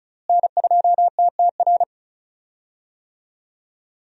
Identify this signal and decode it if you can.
Morse code, heard as a keyed tone, about 36 wpm, 700 Hz. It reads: N2TTR
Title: USA Call Signs - 5 Characters - Repeated 2x - 35wpm